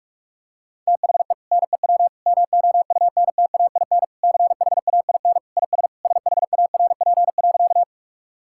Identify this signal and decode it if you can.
Morse code, heard as a keyed tone, about 46 wpm, 700 Hz. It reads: THE NEW MOUNTAIN CHAIN IS SHARP.
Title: Sentences from Top 1600 Words 45wpm